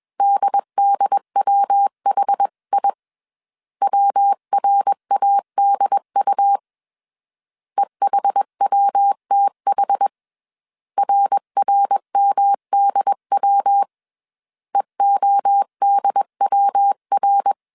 Morse code, heard as a keyed tone, about 21 wpm, 800 Hz. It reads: DDWHI WRADU EHWTH RRMDW EODWR
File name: CW / DET / 1 3 1500C S B 20wpm 1754